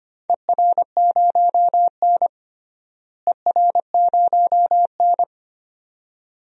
Morse code, heard as a keyed tone, about 25 wpm, 700 Hz. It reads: ER0N ER0N
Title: DXCC-100 Call Signs - 4 Characters - Repeated 2x - 25wpm